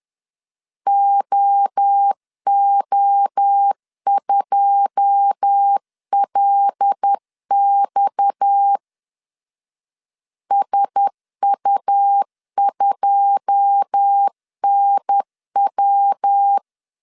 Morse code, heard as a keyed tone, about 10 wpm, 800 Hz. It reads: OO2LX SU2NW
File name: LEZIONI / 9 2 10wpm 2736